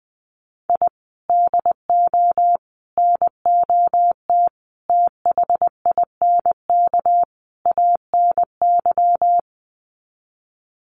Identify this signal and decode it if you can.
Morse code, heard as a keyed tone, about 20 wpm, 700 Hz. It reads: I DO NOT THINK ANY
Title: Sets of 5 Words Encore 20wpm